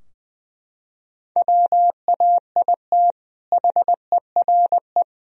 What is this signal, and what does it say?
Morse code, heard as a keyed tone, about 20 wpm, 700 Hz. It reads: WAIT HERE
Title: Sentences from Top 1000 Words - Review Part 4 - 20wpm